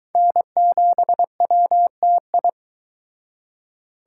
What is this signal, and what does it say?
Morse code, heard as a keyed tone, about 22 wpm, 700 Hz. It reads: N7WTI